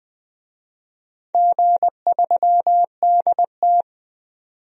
Morse code, heard as a keyed tone, about 20 wpm, 700 Hz. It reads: G3DT
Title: DXCC-100 Call Signs - 4 Characters - 20wpm